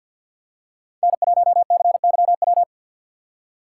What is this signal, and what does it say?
Morse code, heard as a keyed tone, about 47 wpm, 700 Hz. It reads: N1CYW
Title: USA Call Signs - 5 Characters - 6x Character Spacing - 50wpm